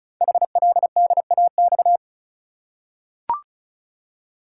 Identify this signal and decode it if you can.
Morse code, heard as a keyed tone, about 36 wpm, 700 Hz. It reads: HLDAX
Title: Code Groups - 5 Letters - 35wpm